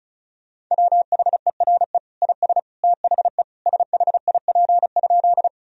Morse code, heard as a keyed tone, about 34 wpm, 700 Hz. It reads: WHERE IS THE SHIP?
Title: Sentences from Top 1000 Words - Review Part 2 - 35wpm